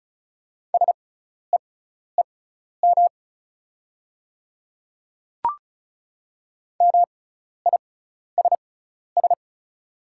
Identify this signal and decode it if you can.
Morse code, heard as a keyed tone, about 34 wpm, spaced as for 12 wpm, 700 Hz. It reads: SEEM MISS
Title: 6x Character Spacing - Top 500 Words - 35wpm